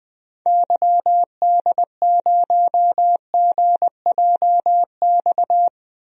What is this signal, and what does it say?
Morse code, heard as a keyed tone, about 20 wpm, 700 Hz. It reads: YD0GJX
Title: DXCC-100 Call Signs - 6 Characters - Repeated 2x - 20wpm